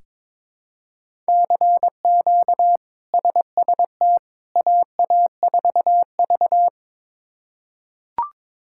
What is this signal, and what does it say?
Morse code, heard as a keyed tone, about 21 wpm, 700 Hz. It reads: CQ SST AA4V
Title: K1USN SST QSO Practice - 22wpm